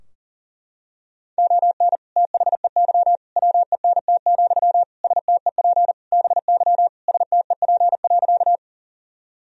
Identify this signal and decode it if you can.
Morse code, heard as a keyed tone, about 40 wpm, 700 Hz. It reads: ON THEY WENT, STEP BY STEP.
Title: Sentences from Top 1000 Words - Review Part 2 - 40wpm